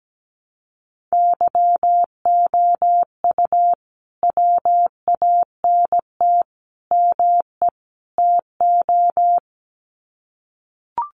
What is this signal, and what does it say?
Morse code, heard as a keyed tone, about 17 wpm, 700 Hz. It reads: YOU WANT ME TO
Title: Sets of 4 Words 17wpm